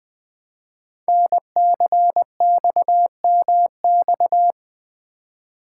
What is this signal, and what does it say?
Morse code, heard as a keyed tone, about 20 wpm, 700 Hz. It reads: NCXMX
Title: Code Groups - 5 Letters - 20wpm